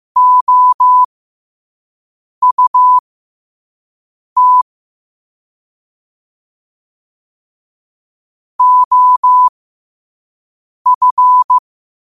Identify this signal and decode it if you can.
Morse code, heard as a keyed tone, about 15 wpm, spaced as for 5 wpm, 1000 Hz. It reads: OUT OF
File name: qotdfiles / 5wpm